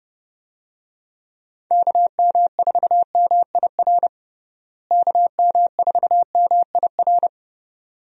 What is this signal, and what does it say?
Morse code, heard as a keyed tone, about 30 wpm, 700 Hz. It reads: KM4MIR KM4MIR
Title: USA Call Signs - 6 Characters - Repeated 2x - 30wpm